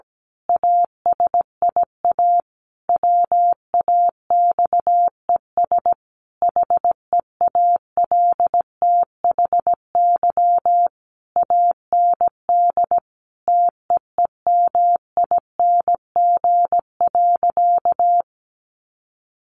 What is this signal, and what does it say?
Morse code, heard as a keyed tone, about 17 wpm, 700 Hz. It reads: ASIA WAXES HEALTHY AND TEEMING.
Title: A Sound Of Thunder 17wpm